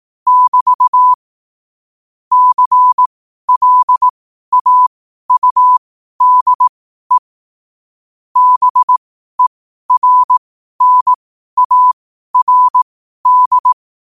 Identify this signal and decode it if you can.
Morse code, heard as a keyed tone, about 18 wpm, spaced as for 12 wpm, 1000 Hz. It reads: <BT> CLAUDE BERNARD
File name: qotdfiles / 12wpm